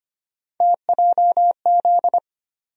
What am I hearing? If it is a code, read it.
Morse code, heard as a keyed tone, about 25 wpm, 700 Hz. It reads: TJZ
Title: USA Call Sign Postfixes - 3 Letters 25wpm